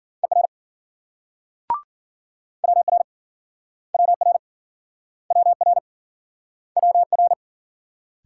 Morse code, heard as a keyed tone, about 54 wpm, 700 Hz. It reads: ER WR WR WR WR